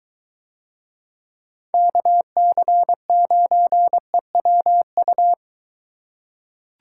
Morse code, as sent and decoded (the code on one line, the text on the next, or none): -.- -.-. ----. .-- ..-
KC9EWU